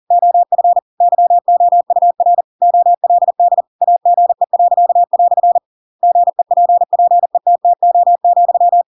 Morse code, heard as a keyed tone, about 40 wpm, 700 Hz. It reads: OF YOUR OLD AGE." GEPPETTO,
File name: Pinocchio / Pinocchio ch8 40 40 ik1hge